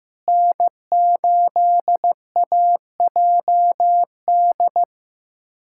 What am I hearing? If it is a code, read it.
Morse code, heard as a keyed tone, about 15 wpm, 700 Hz. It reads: N8AJD
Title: USA Callsign DE Callsign - 5 Characters - 15wpm